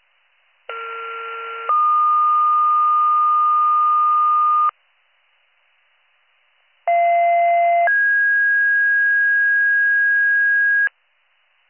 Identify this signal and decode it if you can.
Two-tone page, heard as a one-second tone, then a three-second tone, about 1370 then 1220 Hz, about 690 then 1650 Hz. It is heard from 0.7 to 10.9 s.